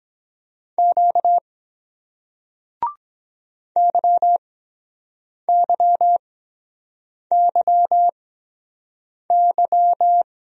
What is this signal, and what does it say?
Morse code, heard as a keyed tone, about 23 wpm, spaced as for 7 wpm, 700 Hz. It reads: Q YYYY